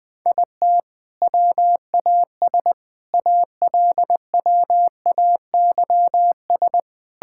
Morse code, heard as a keyed tone, about 21 wpm, 700 Hz. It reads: IT WAS ALWAYS
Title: Sentences from Top 1000 Words - Review Part 5 - 20wpm